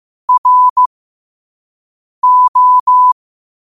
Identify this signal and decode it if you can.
Morse code, heard as a keyed tone, about 15 wpm, spaced as for 5 wpm, 1000 Hz. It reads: RO
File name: qotdfiles / 5wpm